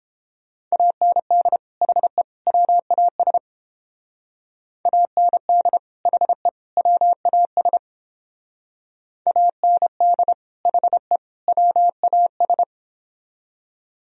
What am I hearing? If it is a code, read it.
Morse code, heard as a keyed tone, about 29 wpm, 700 Hz. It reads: AND HE WAS AND HE WAS AND HE WAS